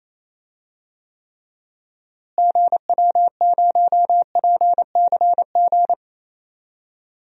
Morse code, heard as a keyed tone, about 28 wpm, 700 Hz. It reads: GW0PCG
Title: DXCC-100 Call Signs - 6 Characters - Repeated 3x - 28wpm